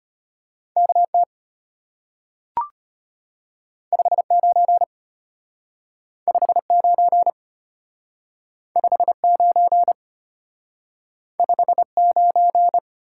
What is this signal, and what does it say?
Morse code, heard as a keyed tone, about 34 wpm, 700 Hz. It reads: KT 59 59 59 59